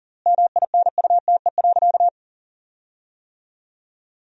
Morse code, heard as a keyed tone, about 40 wpm, 700 Hz. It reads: MINUTE.
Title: 4x Word-Spacing - Sentences from Top 500 Words - 40wpm